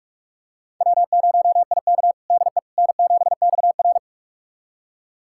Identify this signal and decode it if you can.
Morse code, heard as a keyed tone, about 43 wpm, 700 Hz. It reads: W0IK DE N7XR
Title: USA Callsign DE Callsign - 4 Characters - 45wpm